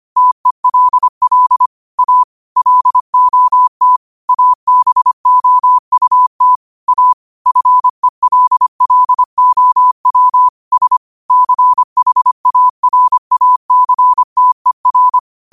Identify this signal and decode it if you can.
Morse code, heard as a keyed tone, about 25 wpm, 1000 Hz. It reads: TELL A LOT ABOUT A FELLOW S CHARACTER
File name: qotdfiles / 25wpm